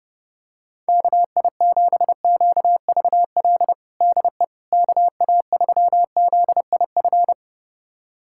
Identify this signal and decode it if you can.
Morse code, heard as a keyed tone, about 32 wpm, 700 Hz. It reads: KI7QVL DE KA3ZIF
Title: USA Callsign DE Callsign 30wpm